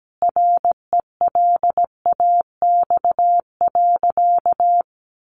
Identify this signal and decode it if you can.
Morse code, heard as a keyed tone, about 17 wpm, 700 Hz. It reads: RELAX.
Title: Sentences from Top 1400 Words 17wpm